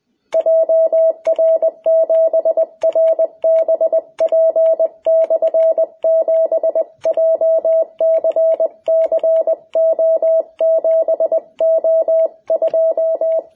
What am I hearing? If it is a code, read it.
Morse code, heard as a keyed tone, about 20 wpm, 660 Hz, with radio noise. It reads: JR7RBP/7 JCCO7O2
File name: jh8cbh / 2021cw